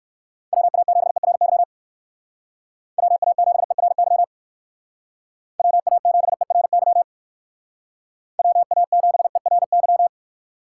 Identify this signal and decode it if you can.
Morse code, heard as a keyed tone, about 56 wpm, 700 Hz. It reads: WA7ERY WA7ERY WA7ERY WA7ERY
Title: Speed Racing - USA 2x3 Call Signs 45wpm